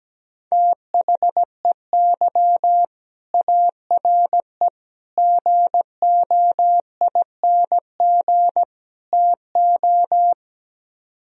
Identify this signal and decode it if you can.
Morse code, heard as a keyed tone, about 17 wpm, 700 Hz. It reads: THEY ARE GOING TO
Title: Speed-Racing - Sets of 4 Words 17wpm